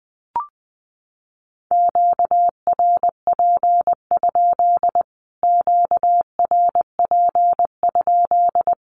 Morse code, heard as a keyed tone, about 20 wpm, 700 Hz. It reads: QRP? QRP?
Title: Q-Codes Repeated 3x 20wpm